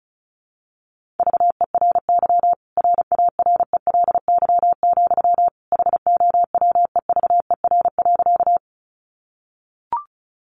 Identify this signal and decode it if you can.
Morse code, heard as a keyed tone, about 34 wpm, 700 Hz. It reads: VERY RARELY, HOWEVER.
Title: Sentences from Top 1400 Words 35wpm